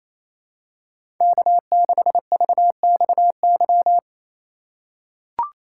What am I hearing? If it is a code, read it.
Morse code, heard as a keyed tone, about 27 wpm, 700 Hz. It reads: K6VXY
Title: USA Call Signs - 6x Character Spacing - 28wpm